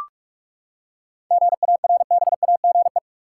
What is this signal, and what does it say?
Morse code, heard as a keyed tone, about 45 wpm, 700 Hz. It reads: GARBAGE
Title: Seven Letter Words - 45wpm